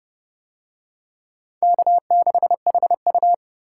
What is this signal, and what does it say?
Morse code, heard as a keyed tone, about 28 wpm, 700 Hz. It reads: K6HU